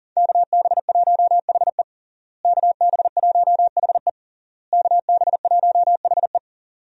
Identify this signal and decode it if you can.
Morse code, heard as a keyed tone, about 40 wpm, 700 Hz. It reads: KB1HE KB1HE KB1HE
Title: USA Call Signs - 5 Characters - Repeated 3x - 40wpm